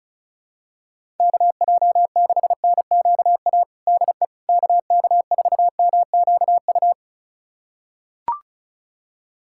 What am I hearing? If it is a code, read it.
Morse code, heard as a keyed tone, about 34 wpm, 700 Hz. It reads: KJ6NQA DE KK4MQU
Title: USA Callsign DE Callsign 35wpm